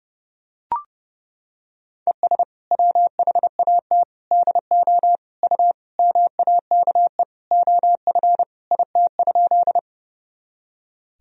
Morse code, heard as a keyed tone, about 30 wpm, 700 Hz. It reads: ES WHAT DO U MAKE OF IT?
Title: Sentences from Top 100 Words and Abbreviations 30wpm